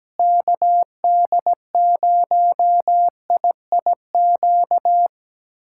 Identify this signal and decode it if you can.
Morse code, heard as a keyed tone, about 17 wpm, 700 Hz. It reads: KD0IIQ
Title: USA Callsign DE Callsign - 6 Characters - 17wpm